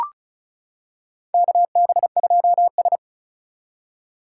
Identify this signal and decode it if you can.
Morse code, heard as a keyed tone, about 36 wpm, 700 Hz. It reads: KB2S
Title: K1USN SST QSO Practice - 35wpm